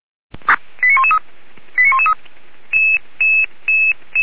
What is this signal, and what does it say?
Five-tone selective call (ZVEI), heard as quick runs of five tones, tones of about 70 ms. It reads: 89113 89113